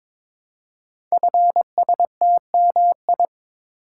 Morse code, heard as a keyed tone, about 23 wpm, 700 Hz. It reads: FSTMI
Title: Code Groups - 5 Letters - 22wpm